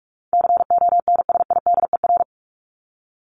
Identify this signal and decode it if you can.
Morse code, heard as a keyed tone, about 44 wpm, 700 Hz. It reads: CONSIDER
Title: ED Suffix - Top 100 Words with Base Words 45wpm